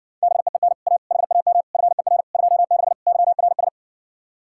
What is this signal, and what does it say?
Morse code, heard as a keyed tone, about 60 wpm, 700 Hz. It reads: BEEN A HAM FER 26 YRS